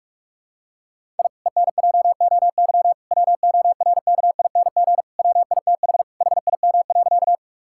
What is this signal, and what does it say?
Morse code, heard as a keyed tone, about 46 wpm, 700 Hz. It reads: I ENJOY WORKING WITH HIM.
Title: Sentences from Top 1000 Words - Review Part 5 - 45wpm